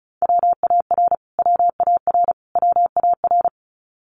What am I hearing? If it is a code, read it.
Morse code, heard as a keyed tone, about 34 wpm, 700 Hz. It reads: WAR WAR WAR
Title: Top 500 Words Repeated 35wpm